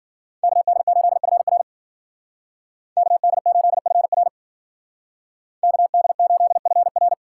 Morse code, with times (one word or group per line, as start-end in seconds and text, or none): 0.4-1.6 KD8FR
3.0-4.3 KD8FR
5.6-7.1 KD8FR